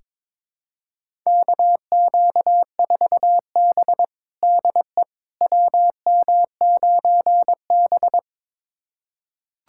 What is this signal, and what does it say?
Morse code, heard as a keyed tone, about 22 wpm, 700 Hz. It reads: KQ4B DE WM9B
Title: USA Callsign DE Callsign - 4 Characters - 22wpm